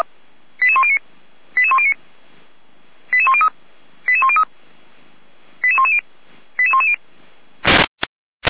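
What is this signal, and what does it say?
Five-tone selective call (ZVEI), heard as quick runs of five tones, tones of about 70 ms. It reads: 88109 88109 88103 88103 88110 88110
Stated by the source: ZVEI five-tone selective call received by SDR, served by Feuerwehr Orlamünde.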